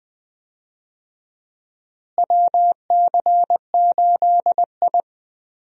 Morse code, heard as a keyed tone, about 20 wpm, 700 Hz. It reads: WC8I
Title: USA Call Signs - 4 Characters - Repeated 2x - 20wpm